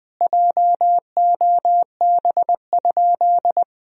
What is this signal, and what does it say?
Morse code, heard as a keyed tone, about 20 wpm, 700 Hz. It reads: JOB?
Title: Sentences from Top 1000 Words - Review Part 3 - 20wpm